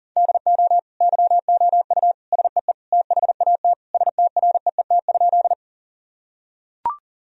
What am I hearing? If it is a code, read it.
Morse code, heard as a keyed tone, about 37 wpm, 700 Hz. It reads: DO YOU SEE THAT STREET?
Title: Sentences from Top 1000 Words - Review Part 3 - 40wpm